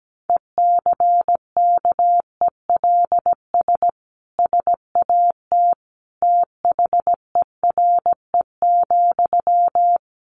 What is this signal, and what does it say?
Morse code, heard as a keyed tone, about 17 wpm, 700 Hz. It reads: ECKELS SAT THERE,
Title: A Sound Of Thunder 17wpm